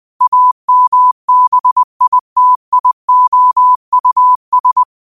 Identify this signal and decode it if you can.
Morse code, heard as a keyed tone, about 20 wpm, 1000 Hz. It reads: AMBITIOUS
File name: qotdfiles / 20wpm